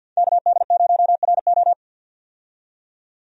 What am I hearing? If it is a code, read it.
Morse code, heard as a keyed tone, about 48 wpm, 700 Hz. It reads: KD0RO
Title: USA Call Signs - 5 Characters - 50wpm